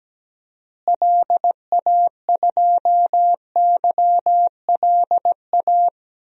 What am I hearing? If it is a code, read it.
Morse code, heard as a keyed tone, about 17 wpm, 700 Hz. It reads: LA2YLA
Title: DXCC-100 Call Signs - 6 Characters - Repeated 3x - 17wpm